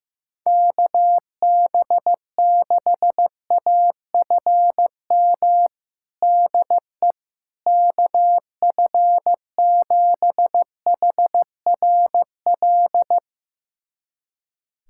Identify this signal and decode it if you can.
Morse code, heard as a keyed tone, about 15 wpm, 700 Hz. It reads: KB6AFM DE KF7HRL
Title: USA Callsign DE Callsign - 6 Characters - 15wpm